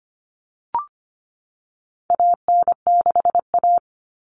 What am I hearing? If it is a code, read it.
Morse code, heard as a keyed tone, about 25 wpm, 700 Hz. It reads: AN6A
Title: All Cut Numbers - 4-Digit 25wpm